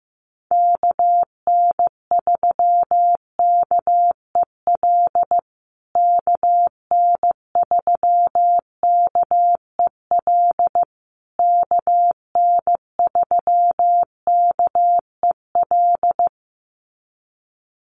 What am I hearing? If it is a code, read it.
Morse code, heard as a keyed tone, about 14 wpm, 700 Hz. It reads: KN3KEL KN3KEL KN3KEL
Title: USA Call Signs Repeated 3x 15wpm